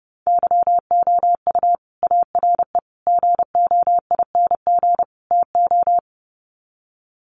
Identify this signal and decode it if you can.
Morse code, heard as a keyed tone, about 30 wpm, 700 Hz. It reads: YOU ARE GOING TO
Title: Sets of 4 Words 30wpm